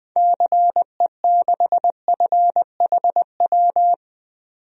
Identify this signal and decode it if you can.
Morse code, heard as a keyed tone, about 20 wpm, 700 Hz. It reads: CE6FHW
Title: DXCC-100 Call Signs - 6 Characters - 20wpm